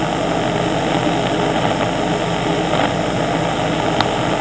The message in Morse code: none